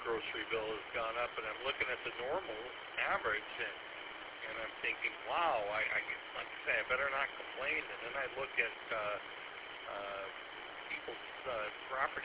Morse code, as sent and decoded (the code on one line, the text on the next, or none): none